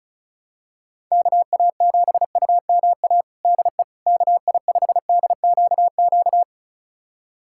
Morse code, heard as a keyed tone, about 36 wpm, 700 Hz. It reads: KA7UMA DE KI5DQQ
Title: USA Callsign DE Callsign - 6 Characters - 35wpm